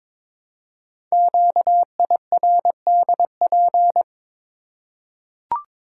Morse code, heard as a keyed tone, about 22 wpm, 700 Hz. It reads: QIRDP
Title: Code Groups - 5 Letter:Numbers - 22wpm